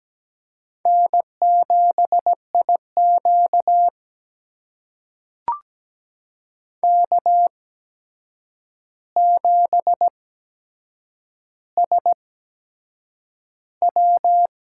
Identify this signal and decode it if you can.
Morse code, heard as a keyed tone, about 17 wpm, spaced as for 7 wpm, 700 Hz. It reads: N7IQ K7SW
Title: USA Call Signs - 4 Characters - 8x Character Spacing - 17wpm